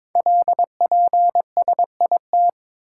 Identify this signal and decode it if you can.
Morse code, heard as a keyed tone, about 22 wpm, 700 Hz. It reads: LPSIT
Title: Code Groups - 5 Letters - 22wpm